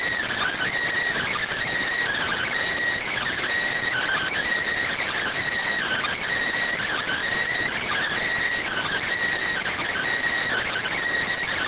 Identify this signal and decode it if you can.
Emergency Alert System (SAME) tones (header bursts), heard as harsh buzzing bursts. It is heard from 0.3 to 10.9 s.